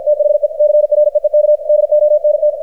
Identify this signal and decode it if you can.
Morse code, heard as a keyed tone, about 53 wpm, 600 Hz. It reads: THE CREEK DONT
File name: archived / CRSnet